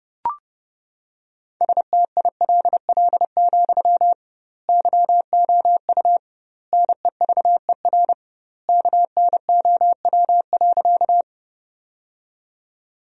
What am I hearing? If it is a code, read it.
Morse code, heard as a keyed tone, about 30 wpm, 700 Hz. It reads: STILL, YOU NEVER KNOW.
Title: 2x Word-Spacing - Sentences from Top 300 Words - 30wpm